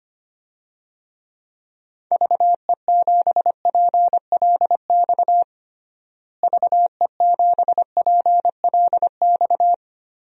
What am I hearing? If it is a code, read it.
Morse code, heard as a keyed tone, about 25 wpm, 700 Hz. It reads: VE7PLX VE7PLX
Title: DXCC-100 Call Signs - 6 Characters - Repeated 2x - 25wpm